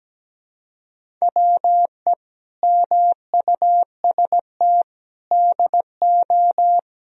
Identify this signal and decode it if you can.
Morse code, heard as a keyed tone, about 17 wpm, 700 Hz. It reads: WE MUST DO
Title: Sentences from Top 1000 Words - Review Part 3 - 17wpm